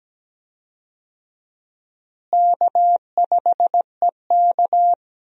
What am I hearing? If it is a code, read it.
Morse code, heard as a keyed tone, about 17 wpm, 700 Hz. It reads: K5EK
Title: USA Call Signs - 4 Characters - Repeated 3x - 17wpm